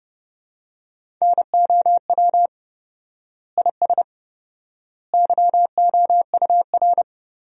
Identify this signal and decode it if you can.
Morse code, heard as a keyed tone, about 30 wpm, 700 Hz. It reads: NOW IS YOUR
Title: 4x Word-Spacing - Sentences from Top 100 Words - 30wpm